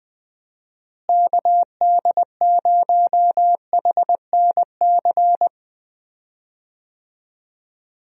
Morse code, heard as a keyed tone, about 20 wpm, 700 Hz. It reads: KD0HNC
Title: USA Call Signs - 6 Characters - Repeated 2x - 20wpm